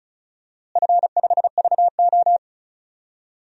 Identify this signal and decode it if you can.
Morse code, heard as a keyed tone, about 36 wpm, 700 Hz. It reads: F5VO
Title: DXCC-100 Call Signs - 4 Characters - Repeated 3x - 35wpm